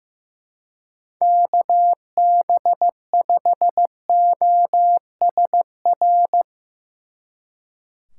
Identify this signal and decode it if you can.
Morse code, heard as a keyed tone, about 16 wpm, 700 Hz. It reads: KB5OSR
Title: Speed Racing - USA 2x3 Call Signs 15wpm